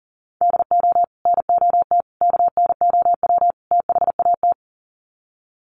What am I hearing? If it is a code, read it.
Morse code, heard as a keyed tone, about 37 wpm, 700 Hz. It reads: DO NOT KNOW THAT
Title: Sets of 4 Words 40wpm